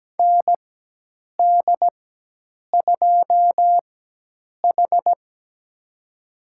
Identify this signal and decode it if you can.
Morse code, heard as a keyed tone, about 17 wpm, spaced as for 8 wpm, 700 Hz. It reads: ND2H
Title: USA Call Signs - 4 Characters - 4x Character Spacing - 17wpm